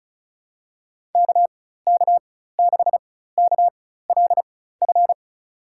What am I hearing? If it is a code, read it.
Morse code, heard as a keyed tone, about 36 wpm, spaced as for 16 wpm, 700 Hz. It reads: KK6KLF
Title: USA Call Signs - 6 Characters - 4x Character Spacing - 35wpm